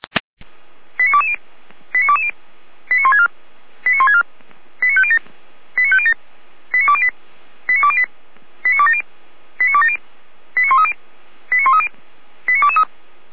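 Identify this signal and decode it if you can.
Five-tone selective call (ZVEI), heard as quick runs of five tones, tones of about 70 ms. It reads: 89220 89220 89164 89164 89507 89507 89208 89208 89270 89270 89130 89130 89223
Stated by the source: ZVEI five-tone selective call received by SDR.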